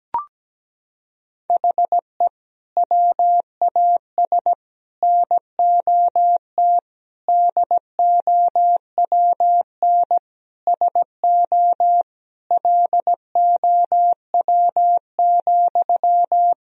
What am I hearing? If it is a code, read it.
Morse code, heard as a keyed tone, about 17 wpm, 700 Hz. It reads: HE WAS NOT DOWN SO LOW,
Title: Sentences from Top 1000 Words - Review Part 1 - 17wpm